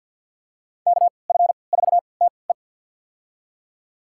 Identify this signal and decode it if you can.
Morse code, heard as a keyed tone, about 49 wpm, spaced as for 28 wpm, 700 Hz. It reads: KF4TE
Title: USA Call Signs - 3x Character Spacing - 50wpm